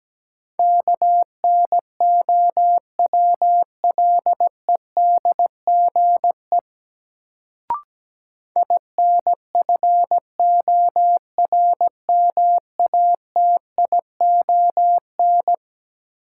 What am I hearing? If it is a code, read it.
Morse code, heard as a keyed tone, about 17 wpm, 700 Hz. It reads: KNOWLEDGE INFORMATION
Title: Sentences from Top 600 Words - New Word Drill - 17wpm